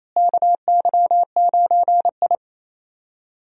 Morse code, heard as a keyed tone, about 28 wpm, 700 Hz. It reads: KY9I
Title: USA Call Signs - 4 Characters - Repeated 3x - 28wpm